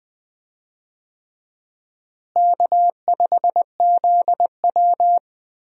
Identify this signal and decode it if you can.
Morse code, heard as a keyed tone, about 21 wpm, 700 Hz. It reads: K5ZW
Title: USA Call Signs - 4 Characters - Repeated 3x - 20wpm